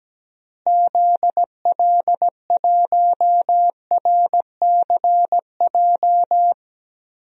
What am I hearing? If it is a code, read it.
Morse code, heard as a keyed tone, about 17 wpm, 700 Hz. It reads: ZL1RCJ
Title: DXCC-100 Call Signs - 6 Characters - Repeated 2x - 17wpm